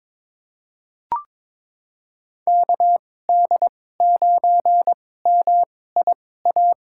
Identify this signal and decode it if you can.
Morse code, heard as a keyed tone, about 22 wpm, spaced as for 16 wpm, 700 Hz. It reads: KD9MIA
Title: USA Call Signs - 6 Characters - 2x Character Spacing - 22wpm